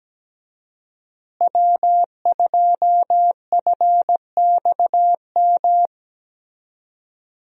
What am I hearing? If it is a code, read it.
Morse code, heard as a keyed tone, about 17 wpm, 700 Hz. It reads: W2FXM
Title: USA Call Signs - 5 Characters - Repeated 2x - 17wpm